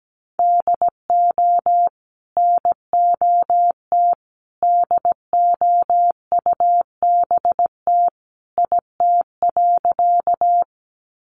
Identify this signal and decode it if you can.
Morse code, heard as a keyed tone, about 17 wpm, 700 Hz. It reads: DO NOT DOUBT IT.
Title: Sentences from Top 1500 Words 17wpm